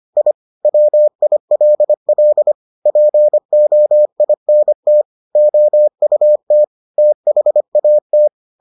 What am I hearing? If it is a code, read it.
Morse code, heard as a keyed tone, about 25 wpm, 600 Hz. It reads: I WILL POINT OUT THAT